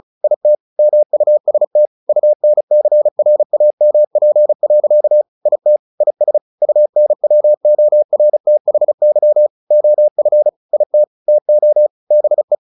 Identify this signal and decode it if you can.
Morse code, heard as a keyed tone, about 34 wpm, 600 Hz. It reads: IT MUST UNCRAMP. IT IS UNWORTHY OF IT TO BE
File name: the old man and the sea / the old man and the sea 1 35wpm